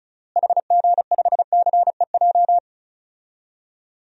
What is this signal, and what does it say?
Morse code, heard as a keyed tone, about 34 wpm, 700 Hz. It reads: HG5CEJ
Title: DXCC-100 Call Signs - 6 Characters - Repeated 3x - 35wpm